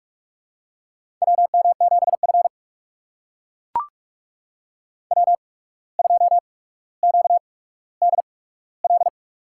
Morse code, heard as a keyed tone, about 44 wpm, spaced as for 18 wpm, 700 Hz. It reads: WM7F W2QDL
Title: USA Call Signs - 8x Character Spacing - 45wpm